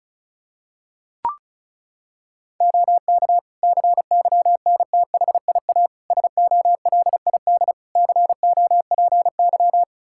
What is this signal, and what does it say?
Morse code, heard as a keyed tone, about 34 wpm, 700 Hz. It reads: OK CYNTHIA SOLID COPY
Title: Practice QSO - Exchange 3 - More Info - 35wpm